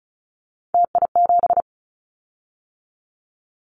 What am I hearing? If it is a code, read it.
Morse code, heard as a keyed tone, about 34 wpm, 700 Hz. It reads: TI7
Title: Callsign Prefixes for Top 100 DXCC - 35wpm